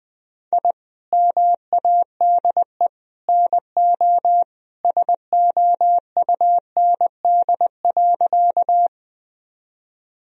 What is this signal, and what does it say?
Morse code, heard as a keyed tone, about 20 wpm, 700 Hz. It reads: I MADE NO SOUND.
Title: Sentences from Top 1000 Words - Review Part 1 - 20wpm